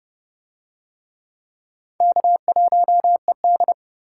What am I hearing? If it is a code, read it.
Morse code, heard as a keyed tone, about 29 wpm, 700 Hz. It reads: K1ED